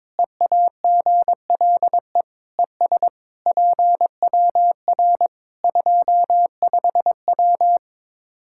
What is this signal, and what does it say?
Morse code, heard as a keyed tone, about 22 wpm, 700 Hz. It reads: EAGLE ES PWR 25W
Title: Practice QSO - Exchange 2 - Info - 22wpm